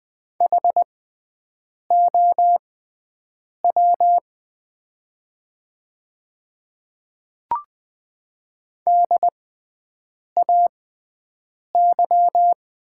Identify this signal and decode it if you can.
Morse code, heard as a keyed tone, about 20 wpm, spaced as for 7 wpm, 690 Hz. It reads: HOW DAY
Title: 6x Character Spacing - Top 100 Words - 20wpm